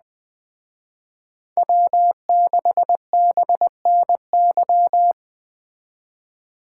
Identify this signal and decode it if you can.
Morse code, heard as a keyed tone, about 20 wpm, 700 Hz. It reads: W6BNY